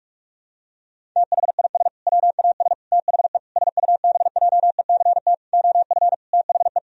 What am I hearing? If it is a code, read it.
Morse code, heard as a keyed tone, about 41 wpm, 700 Hz. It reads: THIS WAS THE SUBJECT OF THE
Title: Sentences from Top 1000 Words - Review Part 1 - 45wpm